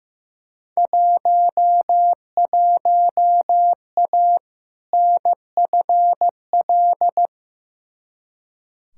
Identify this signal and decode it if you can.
Morse code, heard as a keyed tone, about 15 wpm, 700 Hz. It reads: 11A NFL
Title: Field Day Exchanges Encore - 15wpm